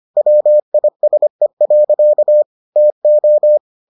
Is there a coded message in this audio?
Yes: Morse code from 0.2 to 3.6 s, heard as a keyed tone, about 25 wpm, 600 Hz. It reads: WISE. TO